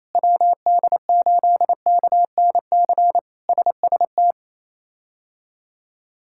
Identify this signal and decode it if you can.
Morse code, heard as a keyed tone, about 28 wpm, 700 Hz. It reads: WD8KNC SST
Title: K1USN SST QSO Practice - 28wpm